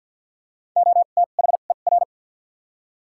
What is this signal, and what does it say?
Morse code, heard as a keyed tone, about 53 wpm, spaced as for 37 wpm, 700 Hz. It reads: OTHER
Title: 2x Character Spacing - Top 100 Words - 50wpm